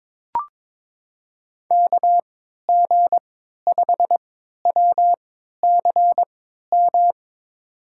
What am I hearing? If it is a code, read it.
Morse code, heard as a keyed tone, about 23 wpm, spaced as for 13 wpm, 700 Hz. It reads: KG5WCM
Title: USA Call Signs - 6 Characters - 3x Character Spacing - 22wpm